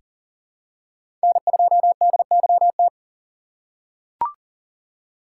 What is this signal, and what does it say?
Morse code, heard as a keyed tone, about 40 wpm, 700 Hz. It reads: N2DYT
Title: K1USN SST QSO Practice - 40wpm